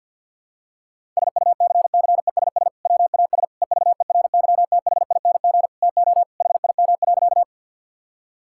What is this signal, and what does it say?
Morse code, heard as a keyed tone, about 52 wpm, 700 Hz. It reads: SUCCESS WAS EVERYTHING TO HIM.
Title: Sentences from Top 1000 Words - Review Part 6 - 50wpm